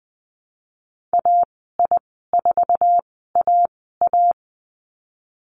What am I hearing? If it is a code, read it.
Morse code, heard as a keyed tone, about 20 wpm, spaced as for 15 wpm, 700 Hz. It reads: AI4AA
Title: 055 Lesson 10g - Letter r - Call Signs 20wpm